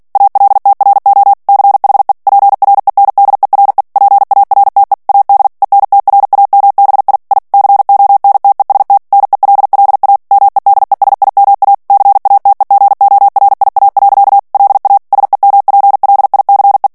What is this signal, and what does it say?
Morse code, heard as a keyed tone, about 48 wpm, 800 Hz. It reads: ALTRO CHE PRENDERE PARTE AD ENTRAMBI I CONTEST NELLA MEDESIMA CATEGORIA. LA SEMPLICE
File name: ITALIEN texte clair 2 vitesse 45 mots minute